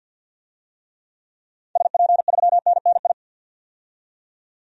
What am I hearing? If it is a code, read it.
Morse code, heard as a keyed tone, about 50 wpm, 700 Hz. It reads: SP3NNI
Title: DXCC-100 Call Signs - 6 Characters - 50wpm